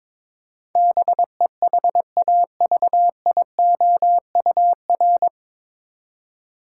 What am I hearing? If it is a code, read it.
Morse code, heard as a keyed tone, about 22 wpm, 700 Hz. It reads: BEHAVIOUR
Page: Morse Code Ninja practice 22wpm